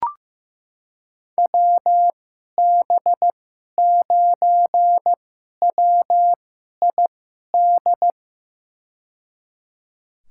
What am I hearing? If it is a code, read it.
Morse code, heard as a keyed tone, about 15 wpm, spaced as for 11 wpm, 700 Hz. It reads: WB9WID
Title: USA Call Signs - 6 Characters - 2x Character Spacing - 15wpm